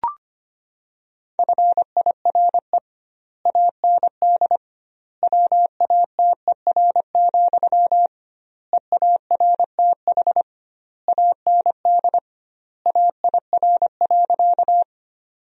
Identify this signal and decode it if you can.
Morse code, heard as a keyed tone, about 25 wpm, 700 Hz. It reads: FIRE AND WATER, EARTH AND AIR.
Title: 2x Word-Spacing - Sentences from Top 400 Words - 25wpm